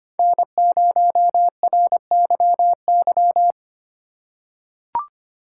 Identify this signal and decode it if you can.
Morse code, heard as a keyed tone, about 25 wpm, 700 Hz. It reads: N0RYY
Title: USA Call Signs - 5 Characters - 4x Character Spacing - 25wpm